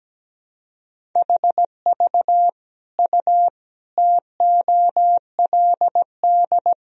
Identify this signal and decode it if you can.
Morse code, heard as a keyed tone, about 17 wpm, 700 Hz. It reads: HV U TOLD
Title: Sentences from Top 400 Words and Abbreviations 17wpm